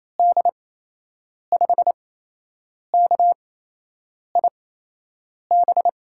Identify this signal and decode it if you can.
Morse code, heard as a keyed tone, about 27 wpm, spaced as for 8 wpm, 700 Hz. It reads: D5KIB
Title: USA Call Signs - 6 Characters - 8x Character Spacing - 28wpm